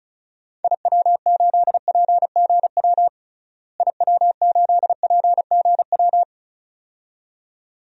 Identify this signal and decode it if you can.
Morse code, heard as a keyed tone, about 36 wpm, 700 Hz. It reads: IW8PGW IW8PGW
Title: DXCC-100 Call Signs - 6 Characters - Repeated 2x - 35wpm